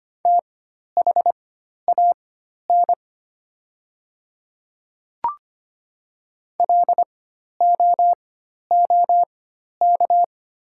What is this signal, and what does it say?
Morse code, heard as a keyed tone, about 25 wpm, spaced as for 12 wpm, 700 Hz. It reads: THAN LOOK
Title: 4x Character Spacing - Top 100 Words - 25wpm